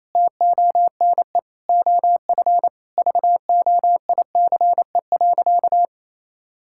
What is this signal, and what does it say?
Morse code, heard as a keyed tone, about 28 wpm, 700 Hz. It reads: TONE OF VOICE.
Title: Sentences from Top 1600 Words 28wpm